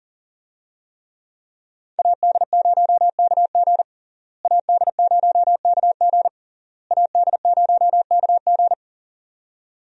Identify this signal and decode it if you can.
Morse code, heard as a keyed tone, about 40 wpm, 700 Hz. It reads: AD0KG AD0KG AD0KG
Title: USA Call Signs - 5 Characters - Repeated 3x - 40wpm